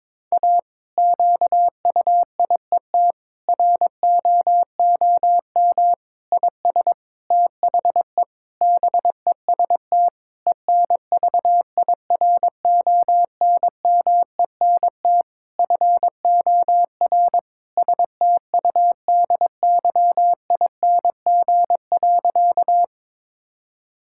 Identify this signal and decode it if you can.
Morse code, heard as a keyed tone, about 22 wpm, 700 Hz. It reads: A QUIET ROOM IS THE BEST ENVIRONMENT FOR STUDYING.
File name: Difficult Sentences with Hints - 22wpm